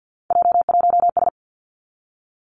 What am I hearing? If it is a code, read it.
Morse code, heard as a keyed tone, about 50 wpm, 700 Hz. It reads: J1S
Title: Three Challenging Letter-Number Combinations 50wpm